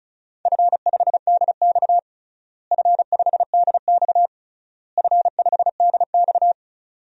Morse code, heard as a keyed tone, about 36 wpm, 700 Hz. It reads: F5DX F5DX F5DX
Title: DXCC-100 Call Signs - 4 Characters - Repeated 3x - 35wpm